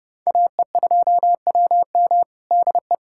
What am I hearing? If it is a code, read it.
Morse code, heard as a keyed tone, about 32 wpm, 700 Hz. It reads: AE2WM DE
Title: USA Callsign DE Callsign - 5 Characters - 30wpm